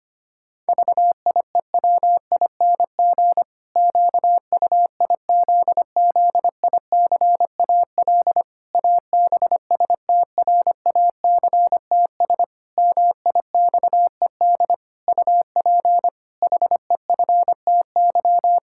Morse code, heard as a keyed tone, about 25 wpm, 700 Hz. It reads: VIEWING QUIZZICAL ABSTRACTS MIXED UP HEFTY
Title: Pangram Challenge v2 - 25wpm